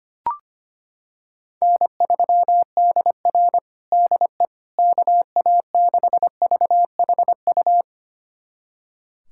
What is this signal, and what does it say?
Morse code, heard as a keyed tone, about 25 wpm, 700 Hz. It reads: N3DR DE KA6VHU
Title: USA Callsign DE Callsign 25wpm